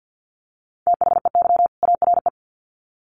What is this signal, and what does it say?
Morse code, heard as a keyed tone, about 50 wpm, 700 Hz. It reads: THEY ARE
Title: Sets of 2 Words 50wpm